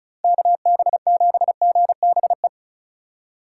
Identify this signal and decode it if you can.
Morse code, heard as a keyed tone, about 34 wpm, 700 Hz. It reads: KB7GBE
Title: USA Call Signs - 6 Characters - 2x Character Spacing - 35wpm